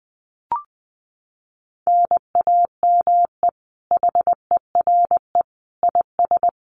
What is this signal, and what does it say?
Morse code, heard as a keyed tone, about 20 wpm, 700 Hz. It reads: NAME HERE IS
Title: Common QSO Phrases 20wpm